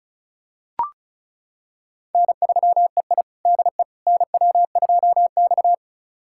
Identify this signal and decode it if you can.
Morse code, heard as a keyed tone, about 36 wpm, 690 Hz. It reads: N3EI DE NW2X
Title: USA Callsign DE Callsign - 4 Characters - 35wpm